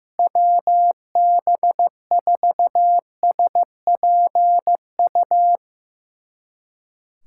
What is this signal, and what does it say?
Morse code, heard as a keyed tone, about 15 wpm, 700 Hz. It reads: WB4SPU
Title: USA Callsign DE Callsign - 6 Characters - 15wpm